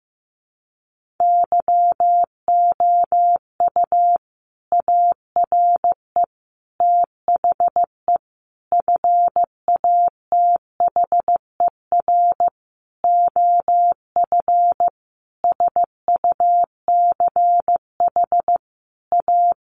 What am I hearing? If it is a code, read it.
Morse code, heard as a keyed tone, about 15 wpm, 700 Hz. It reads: YOU ARE THE FATHER OF SUCH A
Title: Sentences from Top 1500 Words 15wpm